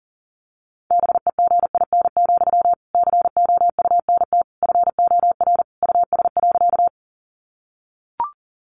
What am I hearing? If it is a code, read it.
Morse code, heard as a keyed tone, about 40 wpm, 700 Hz. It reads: BEGIN, COUNT FOR US.
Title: Sentences from Top 1500 Words 40wpm